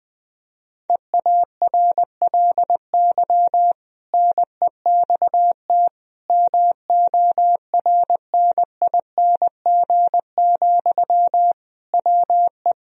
Morse code, heard as a keyed tone, about 20 wpm, 700 Hz. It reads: EARLY NEXT MORNING, WE
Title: Sentences from Top 1000 Words - Review Part 4 - 20wpm